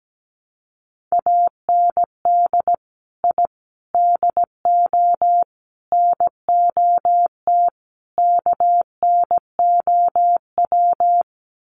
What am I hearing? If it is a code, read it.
Morse code, heard as a keyed tone, about 17 wpm, 700 Hz. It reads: AND I DO NOT KNOW
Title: Sets of 5 Words 17wpm